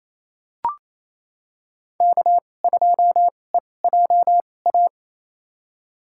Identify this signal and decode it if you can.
Morse code, heard as a keyed tone, about 28 wpm, spaced as for 20 wpm, 700 Hz. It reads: K2EJA
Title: USA Call Signs - 5 Characters - 2x Character Spacing - 28wpm